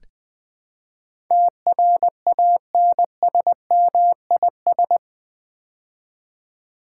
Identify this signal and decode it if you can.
Morse code, heard as a keyed tone, about 20 wpm, 700 Hz. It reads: TRANSMIS
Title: Build A Word - 4 Letters At A Time - 12-Letter Words - 20wpm